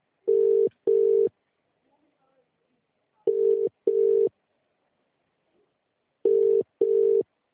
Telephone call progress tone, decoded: ringback tone